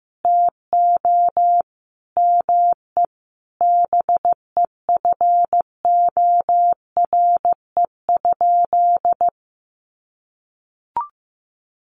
Morse code, heard as a keyed tone, about 15 wpm, 700 Hz. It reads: TO ME BEFORE?
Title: Sentences from Top 900 Words 15wpm